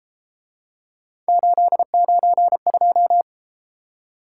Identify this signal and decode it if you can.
Morse code, heard as a keyed tone, about 33 wpm, 700 Hz. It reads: 892